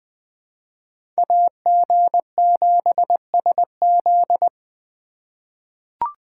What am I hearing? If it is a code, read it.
Morse code, heard as a keyed tone, about 20 wpm, 700 Hz. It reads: AG7SZ
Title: USA Call Signs - 4x Character Spacing - 20wpm